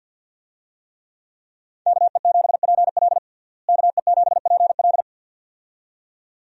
Morse code, heard as a keyed tone, about 50 wpm, 700 Hz. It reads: KE7PL KE7PL
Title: USA Call Signs - 5 Characters - Repeated 2x - 50wpm